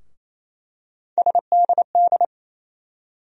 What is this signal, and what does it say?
Morse code, heard as a keyed tone, about 28 wpm, 700 Hz. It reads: SDD
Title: USA Call Sign Postfixes - 3 Letters 28wpm